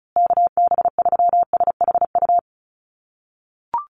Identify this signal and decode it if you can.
Morse code, heard as a keyed tone, about 34 wpm, 700 Hz. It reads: KB3SHU
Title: USA Call Signs Repeated 3x 35wpm